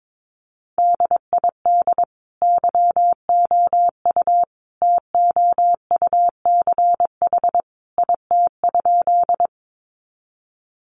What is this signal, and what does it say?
Morse code, heard as a keyed tone, about 22 wpm, 700 Hz. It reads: DID YOU TOUCH IT?
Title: Sentences from Top 1200 Words 22wpm